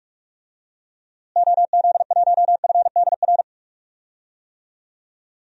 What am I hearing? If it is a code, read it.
Morse code, heard as a keyed tone, about 46 wpm, 700 Hz. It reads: OZ1FDR
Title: DXCC-100 Call Signs - 6 Characters - Repeated 2x - 45wpm